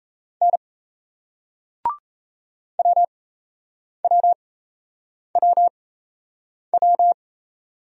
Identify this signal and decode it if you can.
Morse code, heard as a keyed tone, about 35 wpm, spaced as for 8 wpm, 700 Hz. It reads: N WWWW